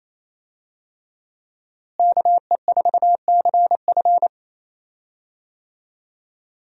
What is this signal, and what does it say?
Morse code, heard as a keyed tone, about 28 wpm, 700 Hz. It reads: KE4CF
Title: USA Call Signs - 5 Characters - Repeated 2x - 28wpm